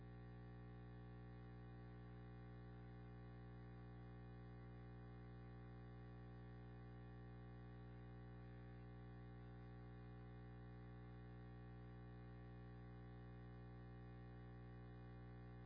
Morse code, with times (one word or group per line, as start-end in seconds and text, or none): none